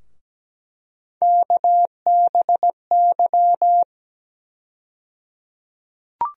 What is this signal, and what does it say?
Morse code, heard as a keyed tone, about 17 wpm, 700 Hz. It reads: KBY
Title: USA Call Sign Postfixes - 3 Letters 17wpm